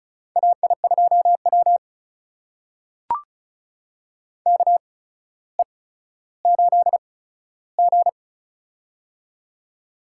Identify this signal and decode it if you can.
Morse code, heard as a keyed tone, about 36 wpm, spaced as for 15 wpm, 700 Hz. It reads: AI2W KE8G
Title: USA Call Signs - 4 Characters - 8x Character Spacing - 35wpm